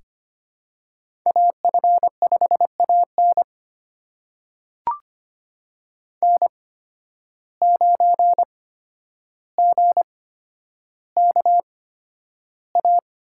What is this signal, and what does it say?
Morse code, heard as a keyed tone, about 25 wpm, spaced as for 11 wpm, 700 Hz. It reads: AF5AN N9GKA
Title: USA Call Signs - 5 Characters - 8x Character Spacing - 25wpm